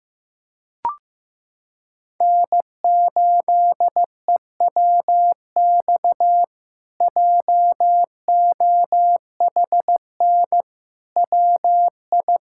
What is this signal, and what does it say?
Morse code, heard as a keyed tone, about 16 wpm, 700 Hz. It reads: N8EWX JOHN WI
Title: K1USN SST QSO Practice - 15wpm